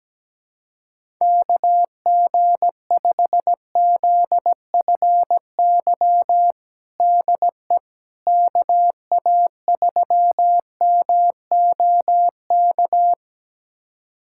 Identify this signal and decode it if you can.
Morse code, heard as a keyed tone, about 17 wpm, 700 Hz. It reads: KG5ZFY DE KA3MOK
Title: USA Callsign DE Callsign - 6 Characters - 17wpm